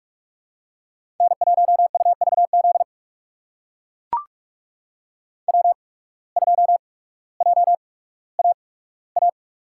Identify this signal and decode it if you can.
Morse code, heard as a keyed tone, about 42 wpm, spaced as for 19 wpm, 700 Hz. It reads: N1UVZ W2JAA
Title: USA Call Signs - 5 Characters - 8x Character Spacing - 45wpm